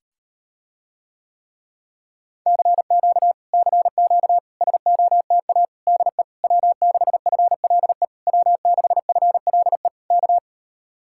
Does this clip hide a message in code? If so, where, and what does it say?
Yes: Morse code from 2.5 to 10.4 s, heard as a keyed tone, about 38 wpm, 700 Hz. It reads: CQ CQ SOTA DE W6FLE W6FLE K